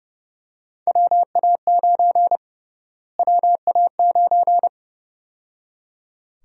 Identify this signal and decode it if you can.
Morse code, heard as a keyed tone, about 30 wpm, 700 Hz. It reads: WA9 WA9
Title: USA Call Sign Prefixes - Repeated 2x - 30wpm